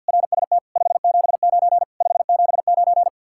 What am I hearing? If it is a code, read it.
Morse code, heard as a keyed tone, about 64 wpm, spaced as for 56 wpm, 700 Hz. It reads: RST 579 579
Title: Speed-Racing Common QSO Phrases - 50wpm